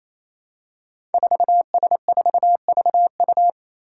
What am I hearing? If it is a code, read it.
Morse code, heard as a keyed tone, about 28 wpm, 700 Hz. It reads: 4S4VU
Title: Five Challenging Letter-Number Combinations 28wpm